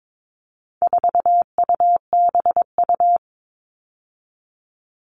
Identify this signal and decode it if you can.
Morse code, heard as a keyed tone, about 22 wpm, 700 Hz. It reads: 4UBU
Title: All Cut Numbers - 4-Digit 22wpm